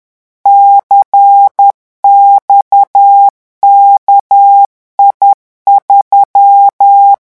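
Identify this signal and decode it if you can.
Morse code, heard as a keyed tone, about 10 wpm, 800 Hz. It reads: CXKI3
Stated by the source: Morse revision exercise for French radio amateurs at 10 words per minute.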